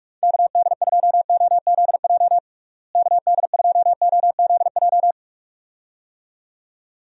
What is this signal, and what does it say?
Morse code, heard as a keyed tone, about 46 wpm, 700 Hz. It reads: KD2OZJ KD2OZJ
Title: USA Call Signs - 6 Characters - Repeated 2x - 45wpm